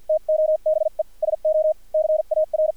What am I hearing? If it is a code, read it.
Morse code, heard as a keyed tone, about 50 wpm, spaced as for 42 wpm, 620 Hz, with radio noise. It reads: TO BE SO CAR
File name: archived / CRSnet